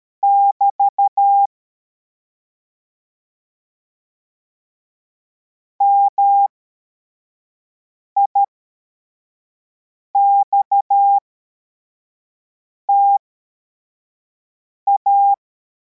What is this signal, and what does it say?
Morse code, heard as a keyed tone, about 13 wpm, spaced as for 4 wpm, 800 Hz. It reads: <BT> MIXTA
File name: formation / cw / Text42